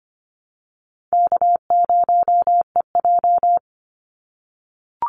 Morse code, heard as a keyed tone, about 25 wpm, 700 Hz. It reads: K0EJ
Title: Call Signs - 25wpm